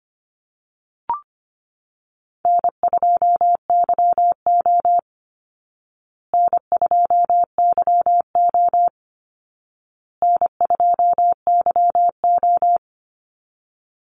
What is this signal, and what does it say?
Morse code, heard as a keyed tone, about 25 wpm, 700 Hz. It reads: N2YO N2YO N2YO
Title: Easy Call Signs Repeated 3x - 25wpm